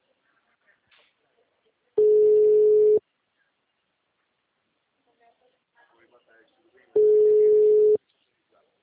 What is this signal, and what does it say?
Ringback tone (Europe, 425 Hz), with speech.